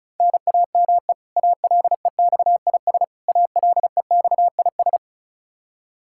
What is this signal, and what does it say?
Morse code, heard as a keyed tone, about 34 wpm, 700 Hz. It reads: NAME ALEXIS ALEXIS
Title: Practice QSO - Exchange 1 - Beginning - 35wpm